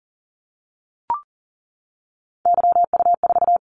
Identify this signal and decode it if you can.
Morse code, heard as a keyed tone, about 42 wpm, 700 Hz. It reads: YU4
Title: Callsign Prefixes for Top 25 DXCC - 40wpm